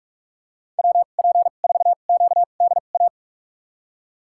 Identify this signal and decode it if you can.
Morse code, heard as a keyed tone, about 42 wpm, spaced as for 31 wpm, 700 Hz. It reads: WP4QDA
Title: USA Call Signs - 6 Characters - 2x Character Spacing - 45wpm